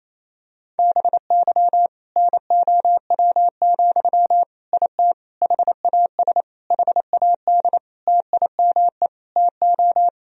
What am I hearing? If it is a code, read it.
Morse code, heard as a keyed tone, about 28 wpm, 700 Hz. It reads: BY NOW, IT HAS HAD TIME TO
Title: Sentences from Top 1000 Words - Review Part 1 - 28wpm